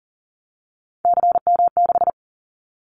Morse code, heard as a keyed tone, about 38 wpm, 700 Hz. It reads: CM6
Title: Callsign Prefixes for Top 100 DXCC - 40wpm